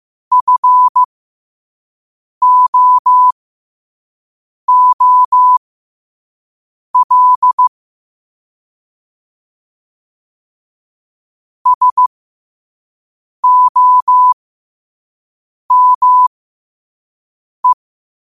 Morse code, heard as a keyed tone, about 15 wpm, spaced as for 5 wpm, 1000 Hz. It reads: FOOL SOME
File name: qotdfiles / 5wpm